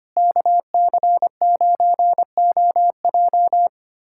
Morse code, heard as a keyed tone, about 25 wpm, 700 Hz. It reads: KC9OJ
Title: USA Callsign DE Callsign - 5 Characters - 25wpm